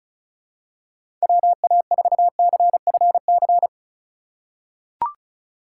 Morse code, heard as a keyed tone, about 34 wpm, 700 Hz. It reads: WA4CFC
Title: USA Call Signs - 6 Characters - 4x Character Spacing - 35wpm